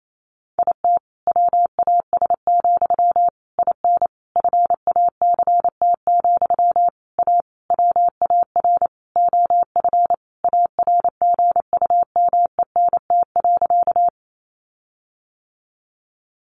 Morse code, heard as a keyed tone, about 28 wpm, 700 Hz. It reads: IT WAS, IN FACT, A WAR OF ARGUMENT.
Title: Sentences from Top 1400 Words 28wpm